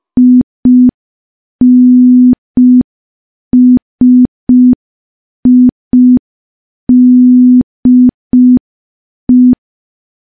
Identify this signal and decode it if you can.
Morse code, heard as a keyed tone, about 5 wpm, 250 Hz. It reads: INSIDE
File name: morse / morse-training-05WPM